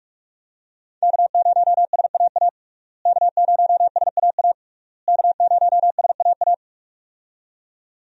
Morse code, heard as a keyed tone, about 41 wpm, 700 Hz. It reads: K0SAA K0SAA K0SAA
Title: USA Call Signs - 5 Characters - Repeated 3x - 45wpm